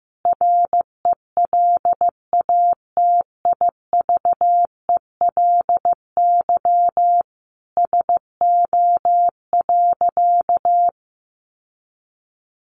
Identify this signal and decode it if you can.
Morse code, heard as a keyed tone, about 14 wpm, 700 Hz. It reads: RELATIVELY SO.
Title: Sentences from Top 1100 Words 15wpm